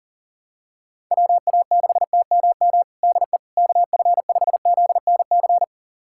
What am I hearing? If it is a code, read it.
Morse code, heard as a keyed tone, about 40 wpm, 700 Hz. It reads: WA6TMM DE KF5ZNC
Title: USA Callsign DE Callsign - 6 Characters - 40wpm